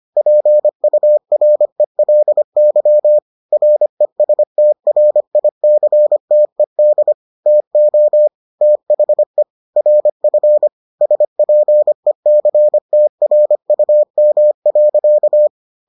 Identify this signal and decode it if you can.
Morse code, heard as a keyed tone, about 25 wpm, 600 Hz. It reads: PURELY RESTRICTED TO THE RF SPECTRUM.